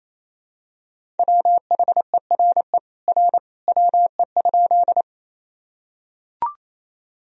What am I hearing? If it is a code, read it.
Morse code, heard as a keyed tone, about 28 wpm, 700 Hz. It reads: WHERE R WE?
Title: Sentences from Top 200 Words and Abbreviations 28wpm